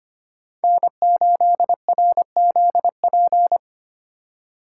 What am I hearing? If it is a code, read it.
Morse code, heard as a keyed tone, about 31 wpm, 700 Hz. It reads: N8RZP
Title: Speed Racing - USA 1x3 Call Signs 25wpm